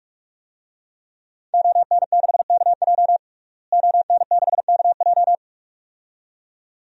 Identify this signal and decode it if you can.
Morse code, heard as a keyed tone, about 41 wpm, 700 Hz. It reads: ON6KJ ON6KJ
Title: DXCC-100 Call Signs - 5 Characters - Repeated 2x - 45wpm